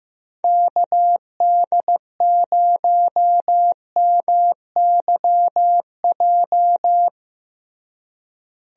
Morse code, heard as a keyed tone, about 16 wpm, 700 Hz. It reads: KD0MYJ
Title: USA Call Signs - 6 Characters - Repeated 2x - 15wpm